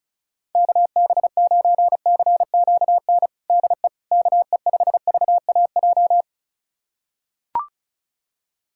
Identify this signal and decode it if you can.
Morse code, heard as a keyed tone, about 36 wpm, 700 Hz. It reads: KB9CQN DE KE5VAJ
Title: USA Callsign DE Callsign - 6 Characters - 35wpm